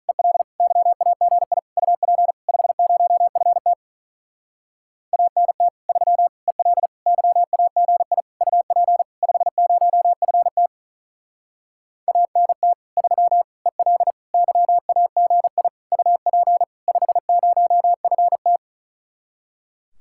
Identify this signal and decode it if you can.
Morse code, heard as a keyed tone, about 43 wpm, 700 Hz. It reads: EL YAGI UP 50FT ANT 3 EL YAGI UP 50FT ANT 3 EL YAGI UP 50FT
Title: Speed-Racing Common QSO Phrases - 35wpm